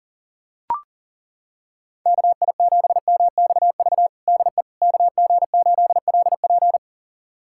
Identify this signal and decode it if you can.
Morse code, heard as a keyed tone, about 40 wpm, 700 Hz. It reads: KI7MXV DE KG8LP
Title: USA Callsign DE Callsign 40wpm